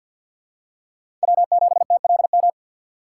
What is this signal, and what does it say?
Morse code, heard as a keyed tone, about 52 wpm, 700 Hz. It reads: W7TLM
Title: USA Call Signs - 5 Characters - 6x Character Spacing - 50wpm